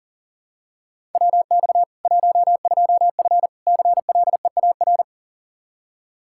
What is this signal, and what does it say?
Morse code, heard as a keyed tone, about 40 wpm, 700 Hz. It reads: WX 12F CLEAR
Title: Practice QSO - Exchange 2 - Info - 40wpm